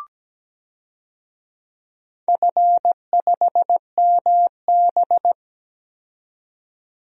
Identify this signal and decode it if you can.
Morse code, heard as a keyed tone, about 17 wpm, 700 Hz. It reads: F5MB
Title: DXCC-100 Call Signs - 4 Characters - 17wpm